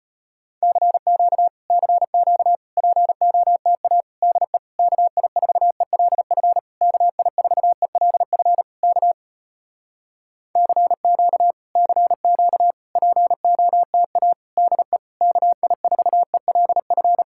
Morse code, heard as a keyed tone, about 34 wpm, 700 Hz. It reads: CQ CQ POTA DE KI4ELF KI4ELF K CQ CQ POTA DE KI4ELF